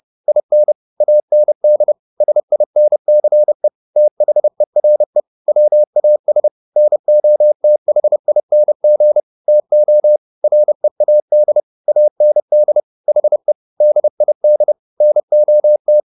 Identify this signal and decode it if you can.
Morse code, heard as a keyed tone, about 30 wpm, 600 Hz. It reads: IN AND SINCE THERE WAS NOTHING TO READ AND HE DID NOT